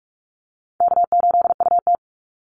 Morse code, heard as a keyed tone, about 44 wpm, 700 Hz. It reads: K8UT
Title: Easy Call Signs Repeated 3x - 45wpm